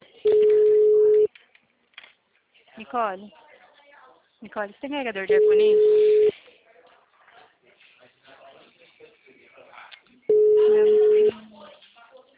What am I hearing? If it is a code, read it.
Ringback tone (Europe, 425 Hz), with speech.